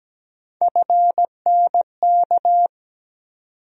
Morse code, heard as a keyed tone, about 17 wpm, 700 Hz. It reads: FNK
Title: USA Call Sign Postfixes - 3 Letters 17wpm